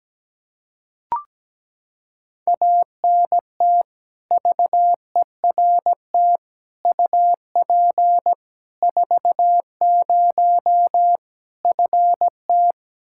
Morse code, heard as a keyed tone, about 17 wpm, 700 Hz. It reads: ANT VERT UP 40 FT
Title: Practice QSO - Exchange 2 - Info - 17wpm